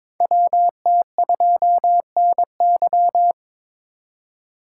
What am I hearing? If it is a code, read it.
Morse code, heard as a keyed tone, about 22 wpm, 700 Hz. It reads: WT2NY
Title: USA Call Signs - 4x Character Spacing - 22wpm